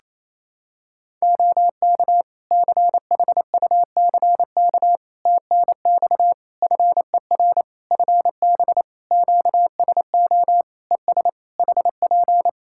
Morse code, heard as a keyed tone, about 27 wpm, 700 Hz. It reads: OK CHUCK TNX FER FB QSO ES HP
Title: Practice QSO - Exchange 4 - Ending - 28wpm